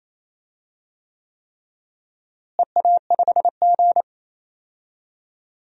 Morse code, heard as a keyed tone, about 28 wpm, 700 Hz. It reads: EA5G
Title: DXCC-100 Call Signs - 4 Characters - Repeated 3x - 28wpm